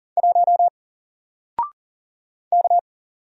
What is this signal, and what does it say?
Morse code, heard as a keyed tone, about 40 wpm, 700 Hz. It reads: 1 K
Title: Single Character (Letter:Number:Punctuation:Prosign) 40wpm